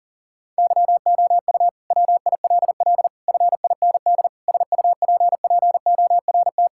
Morse code, heard as a keyed tone, about 40 wpm, 700 Hz. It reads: YOU WILL FIND SUPPORT
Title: Sentences from Top 1000 Words - Review Part 5 - 40wpm